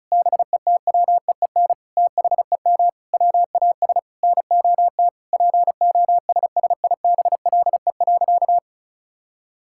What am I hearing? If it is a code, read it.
Morse code, heard as a keyed tone, about 36 wpm, 700 Hz. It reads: BETWEEN THEM WAS NOT POSSIBLE.
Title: Sentences from Top 1000 Words - Review Part 6 - 35wpm